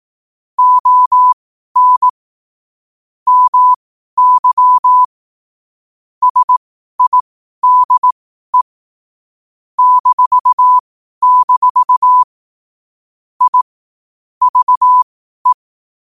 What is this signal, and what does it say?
Morse code, heard as a keyed tone, about 18 wpm, spaced as for 12 wpm, 1000 Hz. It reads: ON MY SIDE -- I VE
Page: Quotes for Wed, 13 Aug 2025 in Morse Code at 12 words per minute.